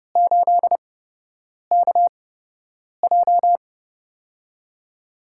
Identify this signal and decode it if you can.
Morse code, heard as a keyed tone, about 31 wpm, spaced as for 8 wpm, 700 Hz. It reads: 8KJ
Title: USA Call Signs - 8x Character Spacing - 30wpm